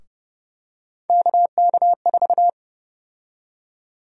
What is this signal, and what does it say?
Morse code, heard as a keyed tone, about 30 wpm, 700 Hz. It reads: KK4
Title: USA Call Sign Prefixes - 30wpm